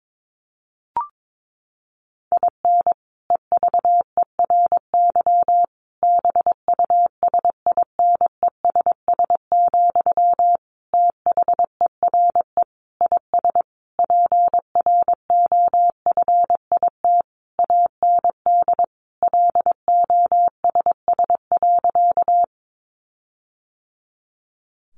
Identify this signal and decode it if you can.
Morse code, heard as a keyed tone, about 22 wpm, 700 Hz. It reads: IN EVERY BUSINESS, THERE IS PROFIT AND LOSS.
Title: Sentences from Top 1400 Words 22wpm